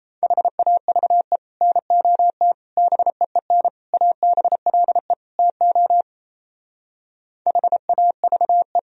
Morse code, heard as a keyed tone, about 36 wpm, 700 Hz. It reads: HAVE NOT BEEN ABLE TO HAVE